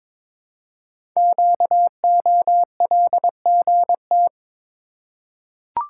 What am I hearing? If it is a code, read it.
Morse code, heard as a keyed tone, about 22 wpm, 700 Hz. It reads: QOLGT
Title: Code Groups - 5 Letters - 22wpm